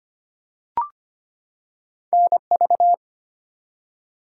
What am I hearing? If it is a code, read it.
Morse code, heard as a keyed tone, about 25 wpm, 700 Hz. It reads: NV
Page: morse-code-ninja-practice-25wpm